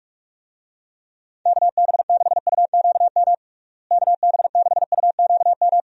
Morse code, heard as a keyed tone, about 44 wpm, 700 Hz. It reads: KB6UQM KB6UQM
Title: USA Call Signs - 6 Characters - Repeated 2x - 45wpm